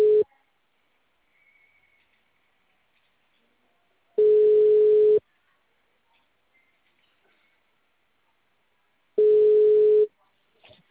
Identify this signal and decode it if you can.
Ringback tone (Europe, 425 Hz), with speech.